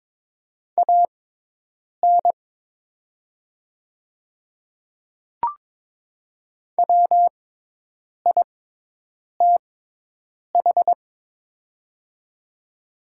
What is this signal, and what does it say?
Morse code, heard as a keyed tone, about 22 wpm, spaced as for 8 wpm, 700 Hz. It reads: AN WITH
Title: 6x Character Spacing - Top 100 Words - 22wpm